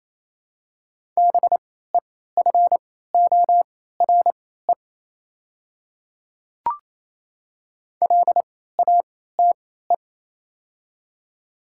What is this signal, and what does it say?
Morse code, heard as a keyed tone, about 28 wpm, spaced as for 16 wpm, 700 Hz. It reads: BEFORE LATE
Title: 3x Character Spacing - Top 300 Words - 28wpm